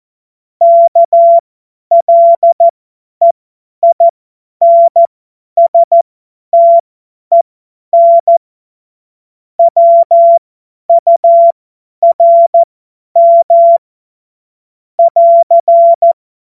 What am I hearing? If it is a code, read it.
Morse code, heard as a keyed tone, about 14 wpm, spaced as for 10 wpm, 680 Hz. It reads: KLEINSTEN WURM <AR>
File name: Morsekurs / Witze / Witz 15. Februar 2022 50 F70